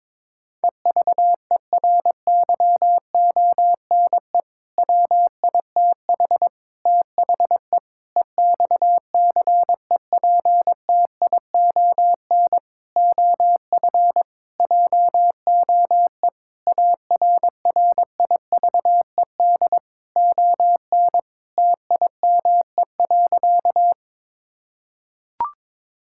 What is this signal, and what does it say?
Morse code, heard as a keyed tone, about 22 wpm, 700 Hz. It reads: EVERYONE WITH THE EXCEPTION OF JOE ARRIVED ON TIME.
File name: practice / 22wpm / Sentences with 4-Word Phrases - 22wpm